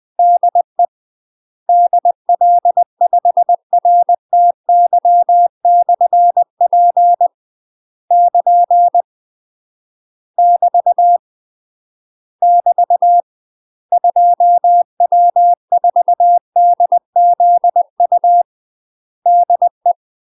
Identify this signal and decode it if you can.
Morse code, heard as a keyed tone, about 20 wpm, 700 Hz. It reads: DE DL5RTY/P <KN> <BT> <BT> 2W4DZU DE
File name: UK MASTER QSOs TRAINING / 20WPM / UK MASTER QSOs 0065